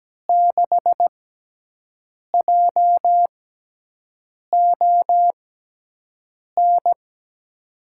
Morse code, heard as a keyed tone, about 17 wpm, spaced as for 6 wpm, 700 Hz. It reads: 6JON
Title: USA Call Signs - 6 Characters - 6x Character Spacing - 17wpm